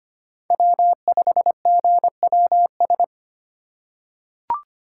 Morse code, heard as a keyed tone, about 25 wpm, 700 Hz. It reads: W5GWS
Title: USA Call Signs - 5 Characters - 2x Character Spacing - 25wpm